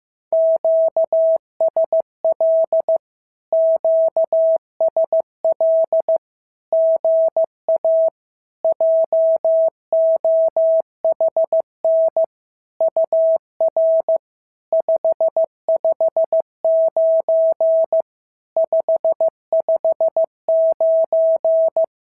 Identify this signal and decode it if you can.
Morse code, heard as a keyed tone, about 15 wpm, 650 Hz. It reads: QSL QSL GA JOHN UR 559 559
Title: SOTA QSO Practice 15wpm